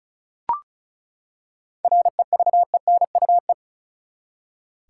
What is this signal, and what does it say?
Morse code, heard as a keyed tone, about 36 wpm, 700 Hz. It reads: REVENUE
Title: Seven Letter Words - 35wpm